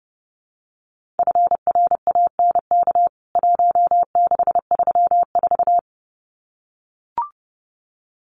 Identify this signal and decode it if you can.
Morse code, heard as a keyed tone, about 30 wpm, 700 Hz. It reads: FRANK 1634
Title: CWT Exchange 30wpm